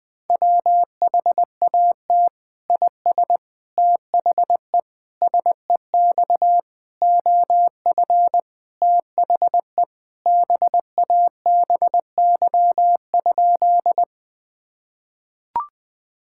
Morse code, heard as a keyed tone, about 20 wpm, 700 Hz. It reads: WHAT IS THE SEX OF THE BABY?
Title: Sentences from Top 1600 Words 20wpm